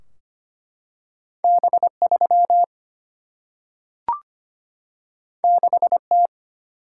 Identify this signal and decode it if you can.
Morse code, heard as a keyed tone, about 25 wpm, 700 Hz. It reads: B3 6T
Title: Single-Letter Single-Number 25wpm